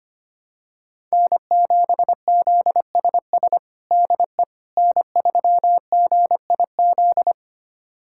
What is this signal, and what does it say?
Morse code, heard as a keyed tone, about 25 wpm, 700 Hz. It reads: N7ZSS DE N3GIZ
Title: USA Callsign DE Callsign - 5 Characters - 25wpm